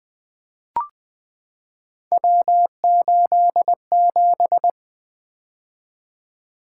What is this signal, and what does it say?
Morse code, heard as a keyed tone, about 20 wpm, 700 Hz. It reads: W87
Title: Three Letter-Number Combinations - 20wpm